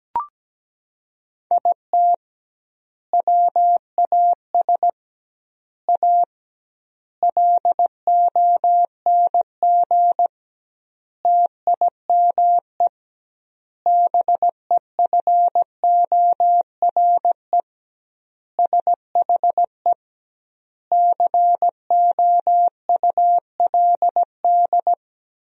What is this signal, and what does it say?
Morse code, heard as a keyed tone, about 17 wpm, 700 Hz. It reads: IT WAS A LONG TIME BEFORE SHE COULD
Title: 2x Word-Spacing - Sentences from Top 500 Words - 17wpm